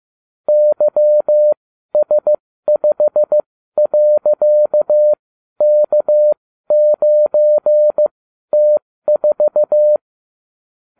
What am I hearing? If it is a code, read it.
Morse code, heard as a keyed tone, about 15 wpm, spaced as for 12 wpm, 600 Hz. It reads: YS5.K9T4
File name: cw / ubacw32 2